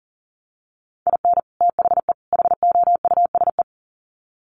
Morse code, heard as a keyed tone, about 40 wpm, 700 Hz. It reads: IN THE HOUSE
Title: Sets of 3 Words 40wpm